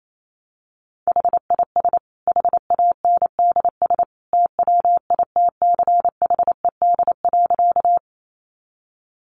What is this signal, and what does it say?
Morse code, heard as a keyed tone, about 28 wpm, 700 Hz. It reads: HIS HANDS TWITCHED.
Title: A Sound Of Thunder 28wpm